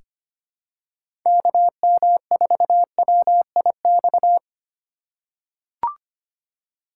Morse code, heard as a keyed tone, about 25 wpm, 700 Hz. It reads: KM4WIX
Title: USA Call Signs - 2x Character Spacing - 25wpm